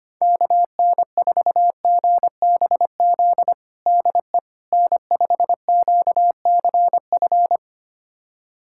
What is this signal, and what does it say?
Morse code, heard as a keyed tone, about 25 wpm, 700 Hz. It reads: KN4GBZ DE N5QCF
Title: USA Callsign DE Callsign 25wpm